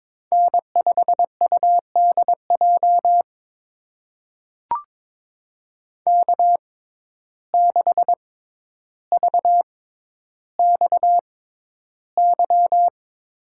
Morse code, heard as a keyed tone, about 22 wpm, spaced as for 11 wpm, 700 Hz. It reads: N5UDJ K6VXY
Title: USA Call Signs - 5 Characters - 6x Character Spacing - 22wpm